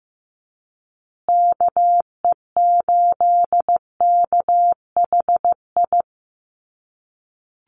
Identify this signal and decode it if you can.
Morse code, heard as a keyed tone, about 15 wpm, 700 Hz. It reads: KE8KHI
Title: USA Call Signs Repeated 2x 15wpm